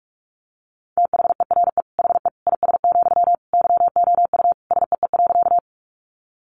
Morse code, heard as a keyed tone, about 44 wpm, 700 Hz. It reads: THERE HE IS, YOU SEE.
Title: Sentences from Top 100 Words 45wpm